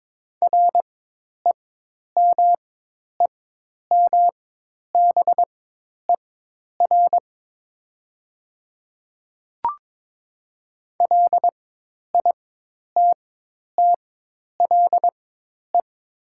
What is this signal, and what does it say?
Morse code, heard as a keyed tone, about 22 wpm, spaced as for 10 wpm, 700 Hz. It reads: REMEMBER LITTLE
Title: 4x Character Spacing - Top 500 Words - 22wpm